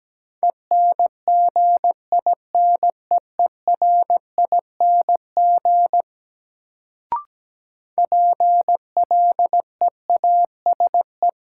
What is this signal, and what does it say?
Morse code, heard as a keyed tone, about 17 wpm, 700 Hz. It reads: ENGINEERING PLEASE
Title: Sentences from Top 1500 Words - New Word Drill - 17wpm